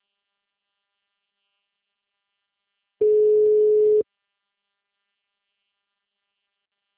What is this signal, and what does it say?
Ringback tone (Europe, 425 Hz).
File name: out- -2018- - - .350